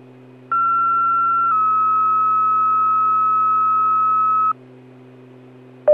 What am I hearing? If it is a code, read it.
Two-tone page, heard as a one-second tone, then a three-second tone, about 1360 then 1290 Hz. It is heard from 0.5 to 4.5 s.